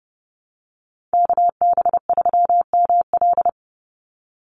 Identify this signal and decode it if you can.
Morse code, heard as a keyed tone, about 30 wpm, 700 Hz. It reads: KB3ML
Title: Call Signs - 30wpm